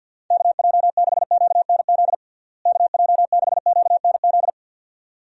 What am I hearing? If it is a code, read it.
Morse code, heard as a keyed tone, about 51 wpm, 700 Hz. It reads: KJ6QNZ KJ6QNZ
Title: USA Call Signs - 6 Characters - Repeated 2x - 50wpm